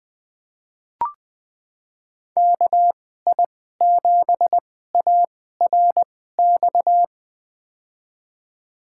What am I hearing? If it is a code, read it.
Morse code, heard as a keyed tone, about 20 wpm, spaced as for 14 wpm, 700 Hz. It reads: KI7ARX
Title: USA Call Signs - 6 Characters - 2x Character Spacing - 20wpm